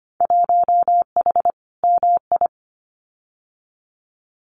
Morse code, heard as a keyed tone, about 25 wpm, 700 Hz. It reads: 1H MI
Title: Winter Field Day Exchanges 25wpm